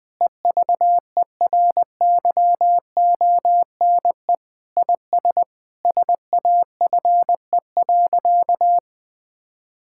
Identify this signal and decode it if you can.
Morse code, heard as a keyed tone, about 20 wpm, 700 Hz. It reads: EVERYONE IS SAFE.
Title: Sentences with 5-Word Phrases - Spoken Beginning - 20wpm